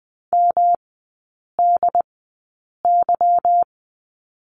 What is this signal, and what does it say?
Morse code, heard as a keyed tone, about 20 wpm, spaced as for 8 wpm, 700 Hz. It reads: MDY
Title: 145 Lesson 24e - Letter y - Three Characters 20wpm